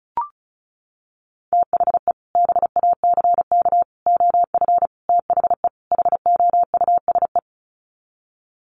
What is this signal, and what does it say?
Morse code, heard as a keyed tone, about 34 wpm, 700 Hz. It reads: THE BACK OF THE HOUSE
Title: Sets of 5 Words Encore2 35wpm